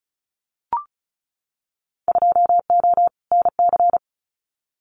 Morse code, heard as a keyed tone, about 34 wpm, 700 Hz. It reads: 2O NC
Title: Winter Field Day Exchanges 35wpm